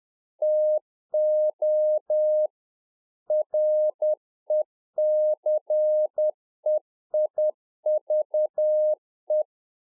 Morse code, heard as a keyed tone, about 10 wpm, 610 Hz. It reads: TO RECEIVE
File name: UNFOGETTABLE QSO 10WPM